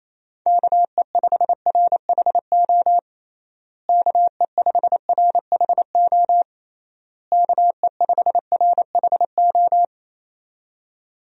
Morse code, heard as a keyed tone, about 28 wpm, 700 Hz. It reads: KE5RHO KE5RHO KE5RHO
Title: USA Call Signs - 6 Characters - Repeated 3x - 28wpm